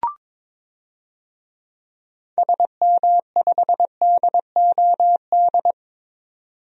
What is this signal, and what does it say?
Morse code, heard as a keyed tone, about 22 wpm, 700 Hz. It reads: SM5DOD
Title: DXCC-100 Call Signs - 6 Characters - 22wpm